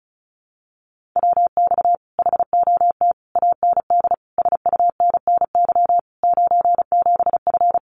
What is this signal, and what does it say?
Morse code, heard as a keyed tone, about 34 wpm, 700 Hz. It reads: WX HOT AND SUNNY 97F
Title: Common QSO Phrases Encore 35wpm